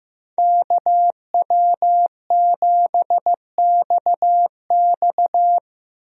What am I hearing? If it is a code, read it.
Morse code, heard as a keyed tone, about 15 wpm, 700 Hz. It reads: KW7XX
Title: USA Call Signs - 5 Characters - 15wpm